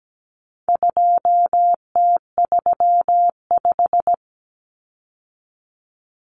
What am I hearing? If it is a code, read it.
Morse code, heard as a keyed tone, about 17 wpm, 700 Hz. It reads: 2T35
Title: Common Cut Numbers - 4-Digit 17wpm